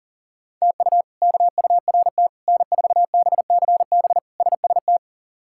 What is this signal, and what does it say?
Morse code, heard as a keyed tone, about 40 wpm, 700 Hz. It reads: TU KURT N4BCB SST
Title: K1USN SST QSO Practice - 40wpm